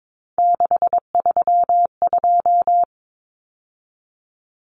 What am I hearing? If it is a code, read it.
Morse code, heard as a keyed tone, about 22 wpm, 700 Hz. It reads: 632
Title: Numbers - 3-Digit 22wpm